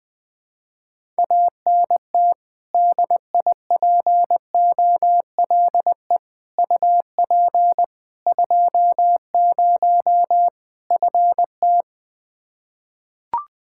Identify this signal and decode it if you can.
Morse code, heard as a keyed tone, about 20 wpm, 700 Hz. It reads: ANT DIPOLE UP 20 FT
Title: Practice QSO - Exchange 2 - Info - 20wpm